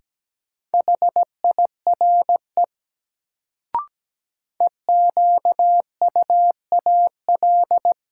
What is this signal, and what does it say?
Morse code, heard as a keyed tone, about 17 wpm, 700 Hz. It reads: HIRE EQUAL
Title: Sentences from Top 1400 Words - New Word Drill - 17wpm